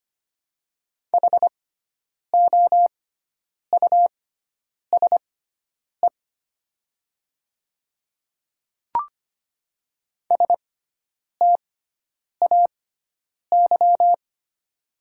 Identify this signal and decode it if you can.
Morse code, heard as a keyed tone, about 25 wpm, spaced as for 9 wpm, 690 Hz. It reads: HOUSE STAY
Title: 6x Character Spacing - Top 500 Words - 25wpm